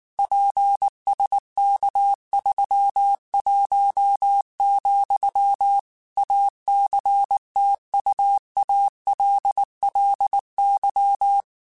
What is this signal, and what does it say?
Morse code, heard as a keyed tone, about 19 wpm, 800 Hz. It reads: PSK31, ACTUALLY
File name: QSOs / QSOs-mp3s / kc7haa@18